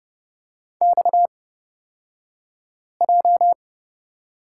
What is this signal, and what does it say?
Morse code, heard as a keyed tone, about 30 wpm, spaced as for 5 wpm, 700 Hz. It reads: XJ
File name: Lesson 37d - Letter x - Two Characters 30wpm